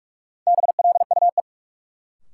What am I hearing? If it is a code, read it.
Morse code, heard as a keyed tone, about 46 wpm, 700 Hz. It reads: BLUE
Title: Sentences from Top 1200 Words - New Word Drill Alt - 45wpm